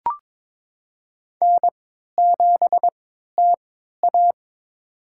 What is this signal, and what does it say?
Morse code, heard as a keyed tone, about 21 wpm, spaced as for 12 wpm, 700 Hz. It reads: N7TA
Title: USA Call Signs - 4 Characters - 3x Character Spacing - 22wpm